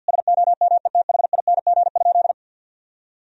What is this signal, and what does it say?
Morse code, heard as a keyed tone, about 50 wpm, 700 Hz. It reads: SOMETHING?
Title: Sentences from Top 1000 Words - Review Part 2 - 50wpm